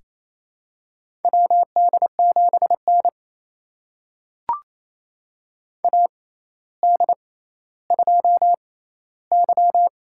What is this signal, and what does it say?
Morse code, heard as a keyed tone, about 28 wpm, spaced as for 14 wpm, 700 Hz. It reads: WD7N AD2Y
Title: USA Call Signs - 4 Characters - 6x Character Spacing - 28wpm